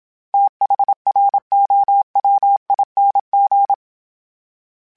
Morse code, heard as a keyed tone, about 27 wpm, 800 Hz. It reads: THROWING
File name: TH / TH mp3s / TH Prefixes 3x2 @25